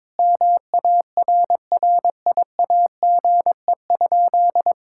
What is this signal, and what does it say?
Morse code, heard as a keyed tone, about 22 wpm, 700 Hz. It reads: MARRIAGE?
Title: Sentences from Top 1000 Words - Review Part 5 - 22wpm